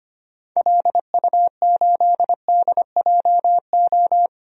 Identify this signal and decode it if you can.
Morse code, heard as a keyed tone, about 25 wpm, 700 Hz. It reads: LU8DJO
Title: DXCC-100 Call Signs - 6 Characters - Repeated 3x - 25wpm